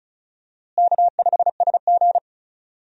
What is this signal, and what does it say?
Morse code, heard as a keyed tone, about 36 wpm, 700 Hz. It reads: K5SG
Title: USA Call Signs - 4 Characters - Repeated 3x - 35wpm